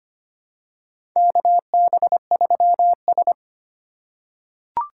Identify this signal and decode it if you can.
Morse code, heard as a keyed tone, about 25 wpm, 700 Hz. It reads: KB3S
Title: USA Call Signs - 4 Characters - 4x Character Spacing - 25wpm